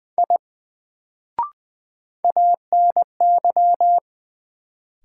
Morse code, heard as a keyed tone, about 20 wpm, 700 Hz. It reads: I ANY
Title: Sentences from Top 200 Words - New Word Drill - 20wpm